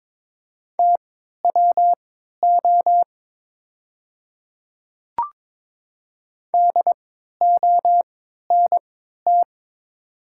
Morse code, heard as a keyed tone, about 23 wpm, spaced as for 13 wpm, 690 Hz. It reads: TWO DONT
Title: 3x Character Spacing - Top 300 Words - 22wpm